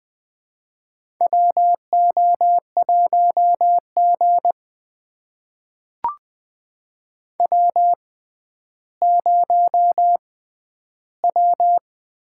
Morse code, heard as a keyed tone, about 20 wpm, spaced as for 11 wpm, 700 Hz. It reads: WO1G W0W
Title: USA Call Signs - 4 Characters - 6x Character Spacing - 20wpm